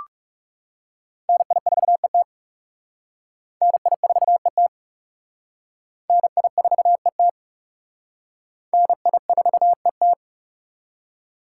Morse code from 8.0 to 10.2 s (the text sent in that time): NI4ET